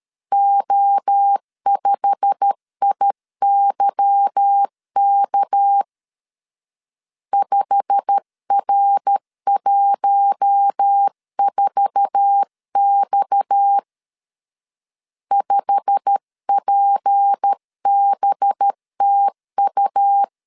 Morse code, heard as a keyed tone, about 13 wpm, 800 Hz. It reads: O5IYK 5R14X 5PBTU
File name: LEZIONI / 8 YJ45 / 8 YJ45 12wpm 2151